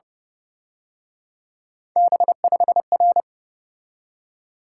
Morse code, heard as a keyed tone, about 31 wpm, 700 Hz. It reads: B5R
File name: Speed-Racing - Three Letter-Number Combinations - 25wpm